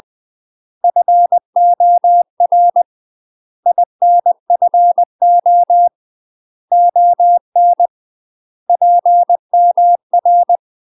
Morse code, heard as a keyed tone, about 20 wpm, 700 Hz. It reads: FOR INFO ON PMR
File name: UK MASTER QSOs TRAINING / 20WPM / UK MASTER QSOs 0046